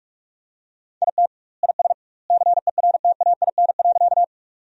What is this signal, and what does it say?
Morse code, heard as a keyed tone, about 41 wpm, 700 Hz. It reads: IT IS CERTAIN.
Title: 2x Word-Spacing - Sentences from Top 500 Words - 45wpm